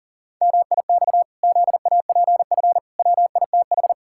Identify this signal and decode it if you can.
Morse code, heard as a keyed tone, about 40 wpm, 700 Hz. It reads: MIX ZAPF WITH
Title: Pangram Challenge v2 - 40wpm